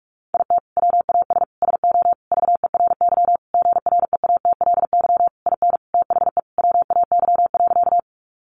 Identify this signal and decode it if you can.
Morse code, heard as a keyed tone, about 44 wpm, 700 Hz. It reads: IT WAS SO VERY GREATLY IN THE WAY.
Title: Sentences from Top 1400 Words 45wpm